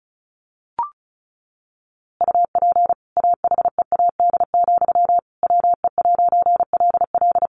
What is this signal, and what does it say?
Morse code, heard as a keyed tone, about 34 wpm, 700 Hz. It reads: UP AHEAD, WE'LL
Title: A Sound Of Thunder 35wpm